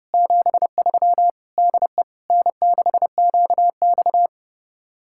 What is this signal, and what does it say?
Morse code, heard as a keyed tone, about 35 wpm, 700 Hz. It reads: 73 DE N6QX
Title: Speed-Racing Common QSO Phrases - 30wpm